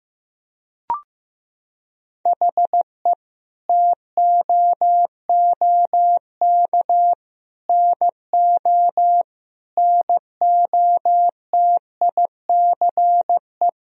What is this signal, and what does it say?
Morse code, heard as a keyed tone, about 15 wpm, 700 Hz. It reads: HE TOOK NO NOTICE
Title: Sentences from Top 1000 Words - Review Part 2 - 15wpm